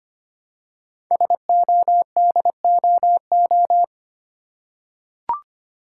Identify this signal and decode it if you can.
Morse code, heard as a keyed tone, about 25 wpm, 700 Hz. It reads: SODOO
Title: Code Groups - 5 Letters - 25wpm